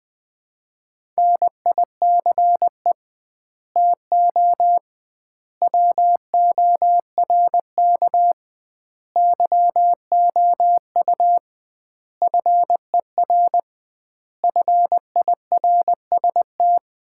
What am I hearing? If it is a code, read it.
Morse code, heard as a keyed tone, about 20 wpm, 700 Hz. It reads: NICE TO WORK YOU FER FIRST
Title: Common QSO Phrases - 2x Word-Spacing - 20wpm